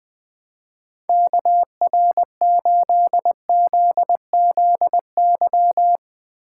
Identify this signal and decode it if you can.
Morse code, heard as a keyed tone, about 20 wpm, 700 Hz. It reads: KR8ZZY
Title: USA Call Signs - 8x Character Spacing - 20wpm